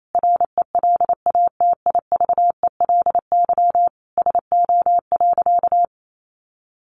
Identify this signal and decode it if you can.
Morse code, heard as a keyed tone, about 28 wpm, 700 Hz. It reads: RELATIVELY SO.
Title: Sentences from Top 1100 Words 28wpm